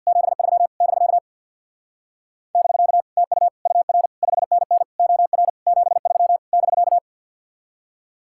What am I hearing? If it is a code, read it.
Morse code, heard as a keyed tone, about 60 wpm, 700 Hz. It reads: 73 <BK> <BK> TU UR 5NN OR 73 <BK>
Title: Speed-Racing Common QSO Phrases - 50wpm